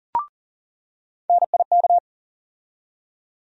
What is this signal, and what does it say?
Morse code, heard as a keyed tone, about 40 wpm, 700 Hz. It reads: NIK
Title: USA Call Sign Postfixes - 3 Letters 40wpm